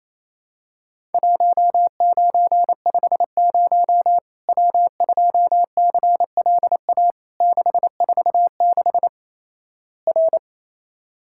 Morse code, heard as a keyed tone, about 28 wpm, 700 Hz. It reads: 1950 W2CLA 646 R
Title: Worked All Europe DX Contest - QTC Exchange - 28wpm